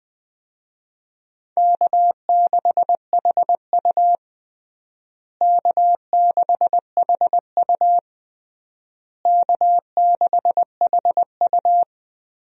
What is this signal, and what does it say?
Morse code, heard as a keyed tone, about 20 wpm, 700 Hz. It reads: K6HU K6HU K6HU
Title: USA Call Signs - 4 Characters - Repeated 3x - 20wpm